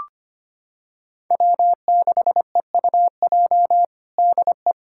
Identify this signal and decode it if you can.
Morse code, heard as a keyed tone, about 25 wpm, 700 Hz. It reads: W6EUJ DE
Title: USA Callsign DE Callsign - 5 Characters - 25wpm